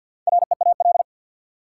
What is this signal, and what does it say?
Morse code, heard as a keyed tone, about 49 wpm, 700 Hz. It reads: REAL
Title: Four Letter Words - 50wpm